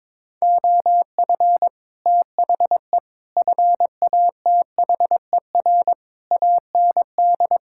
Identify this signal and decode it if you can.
Morse code, heard as a keyed tone, about 22 wpm, 700 Hz. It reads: OF THE FATHER AND
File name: Sets of 9 Words - 22wpm